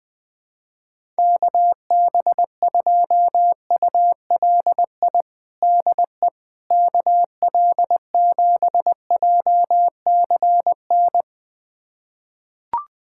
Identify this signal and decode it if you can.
Morse code, heard as a keyed tone, about 21 wpm, 700 Hz. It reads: KB2ULI DE KL7JCN
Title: USA Callsign DE Callsign 20wpm